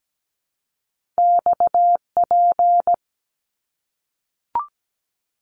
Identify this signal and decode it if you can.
Morse code, heard as a keyed tone, about 17 wpm, 700 Hz. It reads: XP
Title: Top Two Letter Combinations in Words 17wpm